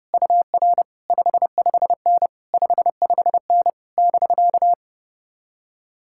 Morse code, heard as a keyed tone, about 33 wpm, 700 Hz. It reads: UR 55N 55N <BK>